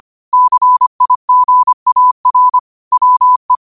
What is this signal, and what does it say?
Morse code, heard as a keyed tone, about 25 wpm, 1000 Hz. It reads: CIGAR WE
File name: qotdfiles / 25wpm